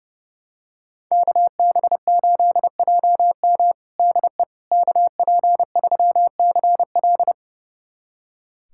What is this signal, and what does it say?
Morse code, heard as a keyed tone, about 30 wpm, 700 Hz. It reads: KB8JM DE KP3CL
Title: USA Callsign DE Callsign - 5 Characters - 30wpm